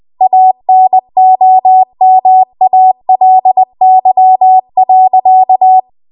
Morse code, heard as a keyed tone, about 20 wpm, 750 Hz. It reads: ANOMALY.
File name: Archive / 20 WPM / 20WPM